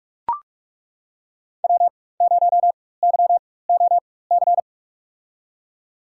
Morse code, heard as a keyed tone, about 41 wpm, spaced as for 20 wpm, 700 Hz. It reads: W0YOC
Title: USA Call Signs - 5 Characters - 4x Character Spacing - 45wpm